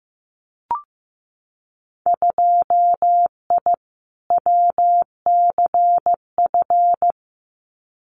Morse code, heard as a keyed tone, about 15 wpm, 700 Hz. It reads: 2I WCF
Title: Winter Field Day Exchanges 15wpm